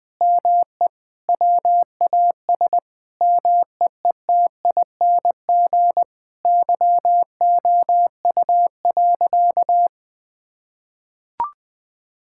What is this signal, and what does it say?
Morse code, heard as a keyed tone, about 20 wpm, 700 Hz. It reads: ME WAS MEETING YOU.
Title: Sentences with 7-Word Phrases - 20wpm